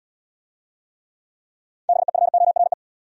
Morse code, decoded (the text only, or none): BEFORE